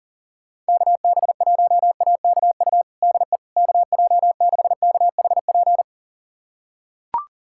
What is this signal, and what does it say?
Morse code, heard as a keyed tone, about 40 wpm, 700 Hz. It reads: KB1AKU DE KJ6KHP
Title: USA Callsign DE Callsign - 6 Characters - 40wpm